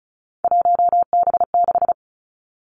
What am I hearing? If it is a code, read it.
Morse code, heard as a keyed tone, about 36 wpm, 700 Hz. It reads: 1B6
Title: Three Challenging Letter-Number Combinations 35wpm